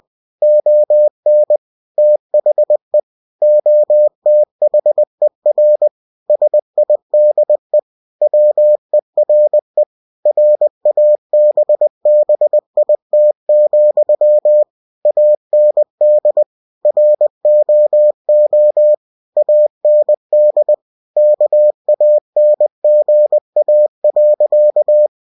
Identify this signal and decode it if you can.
Morse code, heard as a keyed tone, about 20 wpm, 600 Hz. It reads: ON THE OTHER SIDE WERE RABBIT, AND ROO AND KANGA.
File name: cw-books / winnie-the-pooh / winnie-20wpm-page-0106